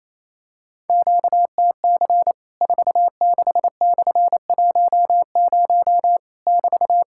Morse code, heard as a keyed tone, about 28 wpm, 700 Hz. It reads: QTC 46/10 <BT>
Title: Worked All Europe DX Contest - QTC Exchange - 28wpm